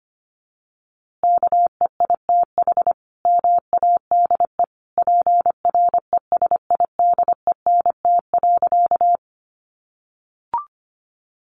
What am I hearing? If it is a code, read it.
Morse code, heard as a keyed tone, about 25 wpm, 700 Hz. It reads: KEITH MADE PRESIDENT.
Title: A Sound Of Thunder 25wpm